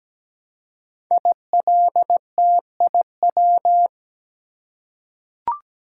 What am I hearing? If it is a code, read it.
Morse code, heard as a keyed tone, about 17 wpm, 700 Hz. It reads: ILTIW
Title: Code Groups - 5 Letters - 17wpm